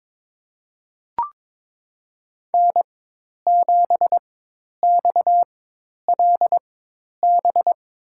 Morse code, heard as a keyed tone, about 22 wpm, spaced as for 10 wpm, 700 Hz. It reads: N7XLB
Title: USA Call Signs - 4x Character Spacing - 22wpm